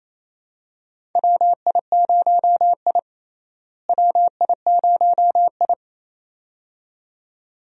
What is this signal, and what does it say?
Morse code, heard as a keyed tone, about 28 wpm, 700 Hz. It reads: WI0I WI0I
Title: USA Call Signs - 4 Characters - Repeated 2x - 28wpm